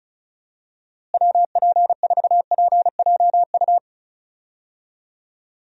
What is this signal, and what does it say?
Morse code, heard as a keyed tone, about 36 wpm, 700 Hz. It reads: WP4PJU
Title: USA Call Signs - 6 Characters - Repeated 2x - 35wpm